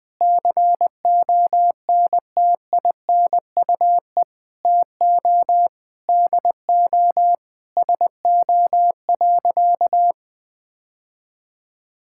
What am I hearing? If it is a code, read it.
Morse code, heard as a keyed tone, about 20 wpm, 700 Hz. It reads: CONTINUE TO DO SO.
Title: Sentences from Top 1000 Words - Review Part 6 - 20wpm